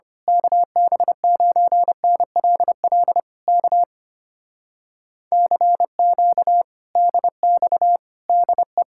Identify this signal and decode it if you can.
Morse code, heard as a keyed tone, about 24 wpm, 700 Hz. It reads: KB9NLL K CQ DX DE
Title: Speed-Racing Common QSO Phrases - 25wpm